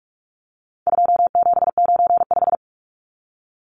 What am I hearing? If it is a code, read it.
Morse code, heard as a keyed tone, about 44 wpm, 700 Hz. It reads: 2795
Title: Numbers - 4-Digit 45wpm